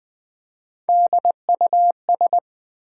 Morse code, heard as a keyed tone, about 20 wpm, 700 Hz. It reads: DUS
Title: Build A Word - 1 Letter At A Time - 3 to 5-Letter Words - 20wpm